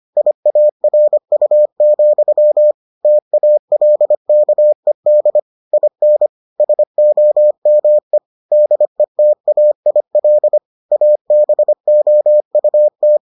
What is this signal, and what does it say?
Morse code, heard as a keyed tone, about 25 wpm, 600 Hz. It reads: IARU, TALKED IN SOME DETAIL ABOUT